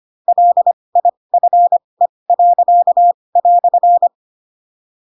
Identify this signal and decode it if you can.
Morse code, heard as a keyed tone, about 25 wpm, spaced as for 20 wpm, 700 Hz. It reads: LIFE."
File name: Pinocchio / Pinocchio ch7 15 25 ik1hge